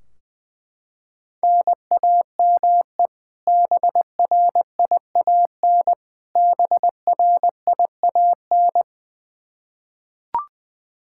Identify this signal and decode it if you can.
Morse code, heard as a keyed tone, about 20 wpm, 700 Hz. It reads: NAME BRIAN BRIAN
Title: K1USN SST QSO Practice 20wpm